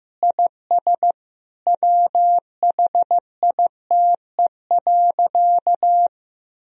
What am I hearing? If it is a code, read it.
Morse code, heard as a keyed tone, about 15 wpm, 700 Hz. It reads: IS WHITE.
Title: Sentences from Top 1000 Words - Review Part 2 - 15wpm